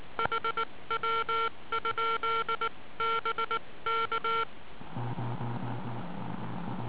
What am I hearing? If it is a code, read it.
Morse code, heard as a keyed tone, about 19 wpm, 1360 Hz. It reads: HW? BK